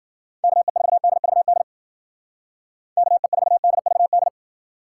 Morse code, heard as a keyed tone, about 53 wpm, 700 Hz. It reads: KE4DVD KE4DVD